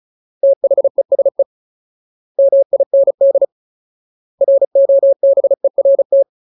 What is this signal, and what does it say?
Morse code, heard as a keyed tone, about 34 wpm, 550 Hz. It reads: THESE MIND ROBERT
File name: Ultimate Word List / 35WPM Ultimate